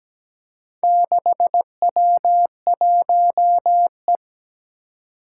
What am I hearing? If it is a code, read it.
Morse code, heard as a keyed tone, about 17 wpm, 700 Hz. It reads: 6W1E
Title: Four Challenging Letter-Number Combinations 17wpm